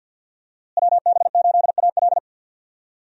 Morse code, heard as a keyed tone, about 51 wpm, 700 Hz. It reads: WB8AL
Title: USA Call Signs - 5 Characters - Repeated 2x - 50wpm